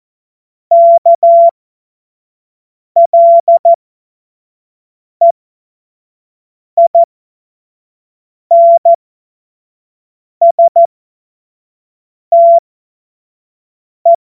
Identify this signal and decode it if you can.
Morse code, heard as a keyed tone, about 14 wpm, spaced as for 5 wpm, 680 Hz. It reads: KLEINSTE
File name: Morsekurs / Witze / Witz 30. August 2022 25 F70